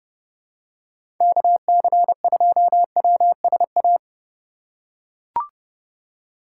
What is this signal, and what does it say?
Morse code, heard as a keyed tone, about 30 wpm, 700 Hz. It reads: KC2WSA
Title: USA Call Signs - 6 Characters - 2x Character Spacing - 30wpm